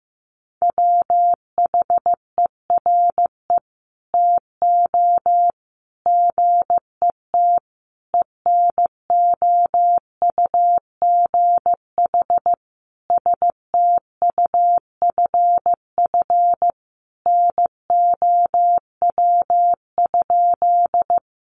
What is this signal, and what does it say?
Morse code, heard as a keyed tone, about 15 wpm, 700 Hz. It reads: WHERE TO GET ENOUGH STUFF NOW?
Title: Sentences from Top 1400 Words 15wpm